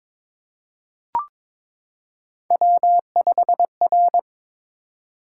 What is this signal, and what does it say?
Morse code, heard as a keyed tone, about 23 wpm, 700 Hz. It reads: W5R
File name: practice / 22wpm / Three Letter-Number Combinations - 22wpm